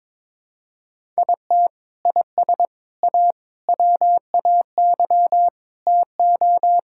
Morse code, heard as a keyed tone, about 22 wpm, 700 Hz. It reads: IT IS A WAY TO
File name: practice / 22wpm / Sentences with 3-Word Phrases - 22wpm